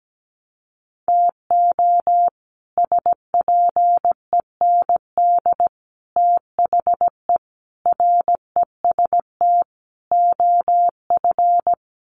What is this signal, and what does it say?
Morse code, heard as a keyed tone, about 17 wpm, 700 Hz. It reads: TO SPEND THE REST OF
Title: Sets of 5 Words Encore2 17wpm